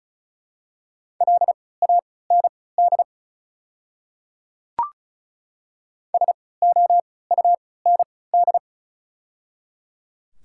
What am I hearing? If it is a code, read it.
Morse code, heard as a keyed tone, about 36 wpm, spaced as for 20 wpm, 700 Hz. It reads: LAND SOUND
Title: 3x Character Spacing - Top 200 Words - 35wpm